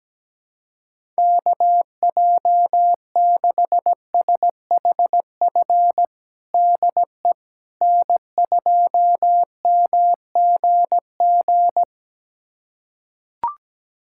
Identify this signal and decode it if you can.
Morse code, heard as a keyed tone, about 17 wpm, 700 Hz. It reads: KJ6SHF DE N2MGG
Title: USA Callsign DE Callsign 17wpm